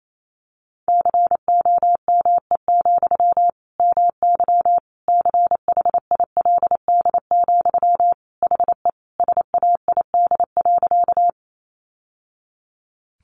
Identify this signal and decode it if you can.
Morse code, heard as a keyed tone, about 28 wpm, 700 Hz. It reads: COME, MY CHILD, HE SAID.
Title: Sentences from Top 800 Words 28wpm